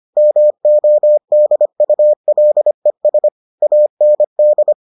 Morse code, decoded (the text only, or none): MODULES AND